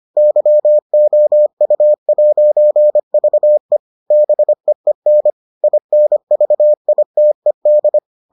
Morse code, heard as a keyed tone, about 25 wpm, 600 Hz. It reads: YOU'VE BEEN INVITED